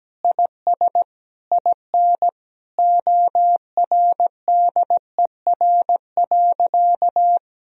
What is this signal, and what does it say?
Morse code, heard as a keyed tone, about 17 wpm, 700 Hz. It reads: IS IN ORDER.
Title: Sentences from Top 1000 Words - Review Part 2 - 17wpm